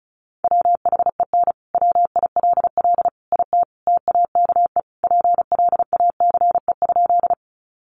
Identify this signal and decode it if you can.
Morse code, heard as a keyed tone, about 34 wpm, 700 Hz. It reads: WHEN WILL IT TAKE PLACE?
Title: Sentences from Top 200 Words 35wpm